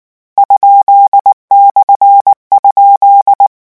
Morse code, heard as a keyed tone, about 19 wpm, 800 Hz. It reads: ?/?
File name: morse mp3 / lecon13-vitesse 18